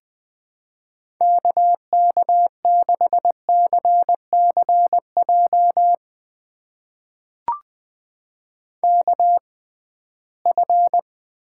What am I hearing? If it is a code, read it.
Morse code, heard as a keyed tone, about 21 wpm, spaced as for 15 wpm, 700 Hz. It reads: KK6CCJ KF
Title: USA Call Signs - 6 Characters - 6x Character Spacing - 20wpm